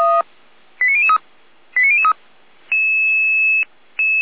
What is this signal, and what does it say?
Five-tone selective call (ZVEI), heard as quick runs of five tones, tones of about 70 ms. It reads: 89003 89003